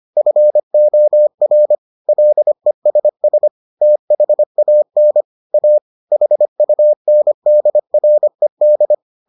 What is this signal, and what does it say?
Morse code, heard as a keyed tone, about 25 wpm, 600 Hz. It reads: FOR LESS THAN A HUNDRED